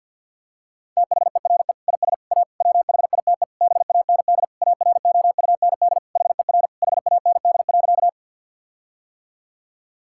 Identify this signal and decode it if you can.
Morse code, heard as a keyed tone, about 50 wpm, 700 Hz. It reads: THERE IS A WHITE BAND AROUND HER HAND.
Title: Sentences from Top 1600 Words 50wpm